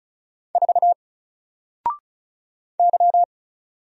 Morse code, heard as a keyed tone, about 36 wpm, 700 Hz. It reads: 4 Y
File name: Single-Letter-Number-35wpm